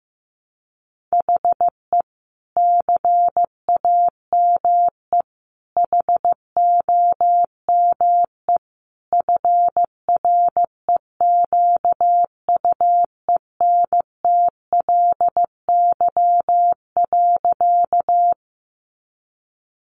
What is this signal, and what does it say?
Morse code, heard as a keyed tone, about 15 wpm, 700 Hz. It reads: HE CAME HOME FREQUENTLY.
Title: Sentences from Top 1300 Words 15wpm